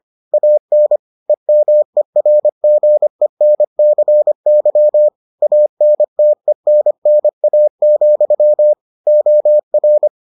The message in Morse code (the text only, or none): AN EMERGENCY ANTENNA, OR